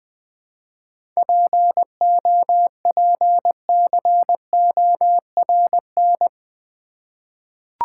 Morse code, heard as a keyed tone, about 20 wpm, 700 Hz. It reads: POPCORN
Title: Easy Words with Hints - 20wpm